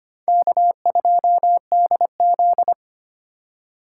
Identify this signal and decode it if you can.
Morse code, heard as a keyed tone, about 25 wpm, 700 Hz. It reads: K2DZ
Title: USA Call Signs - 4x Character Spacing - 25wpm